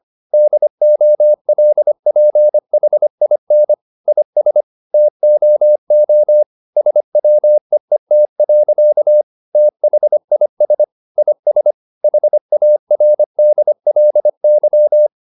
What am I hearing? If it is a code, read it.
Morse code, heard as a keyed tone, about 25 wpm, 600 Hz. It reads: DOLPHIN IS TOO SWEET. THIS IS HARDLY
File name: the old man and the sea part 1 25wpm